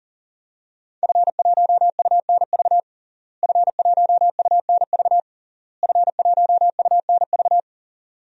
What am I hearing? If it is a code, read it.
Morse code, heard as a keyed tone, about 37 wpm, 700 Hz. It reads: F1UNV F1UNV F1UNV
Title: DXCC-100 Call Signs - 5 Characters - Repeated 3x - 40wpm